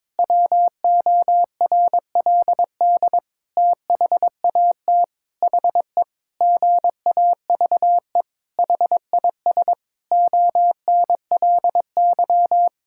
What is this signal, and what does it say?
Morse code, heard as a keyed tone, about 22 wpm, 700 Hz. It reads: WORLD THAT HE GAVE HIS ONLY
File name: practice / 22wpm / Sentences with 8-Word Phrases - 22wpm